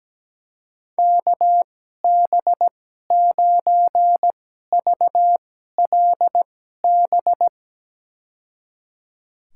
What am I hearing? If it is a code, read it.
Morse code, heard as a keyed tone, about 17 wpm, spaced as for 12 wpm, 700 Hz. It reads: KB9VLB
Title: USA Call Signs - 6 Characters - 2x Character Spacing - 17wpm